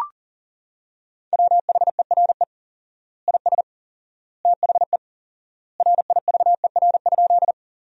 Morse code, heard as a keyed tone, about 37 wpm, 700 Hz. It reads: WHERE IS THE RIVER?
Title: 4x Word-Spacing - Sentences from Top 300 Words - 40wpm